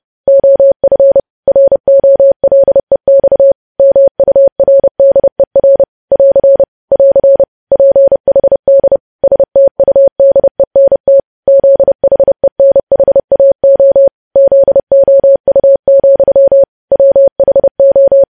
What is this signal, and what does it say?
Morse code, heard as a keyed tone, about 30 wpm, 550 Hz. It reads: OF ROLEX MURDER <AR> <AR> PHD STUDENT ZHENHAO ZOU, WHO
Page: news headlines in morse code at 30 wpm